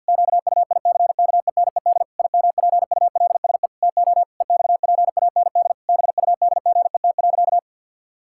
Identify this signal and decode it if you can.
Morse code, heard as a keyed tone, about 50 wpm, 700 Hz. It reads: QUICKENED IMPULSE TO EXPAND BUDGET.
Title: Pangram Challenge v2 - 50wpm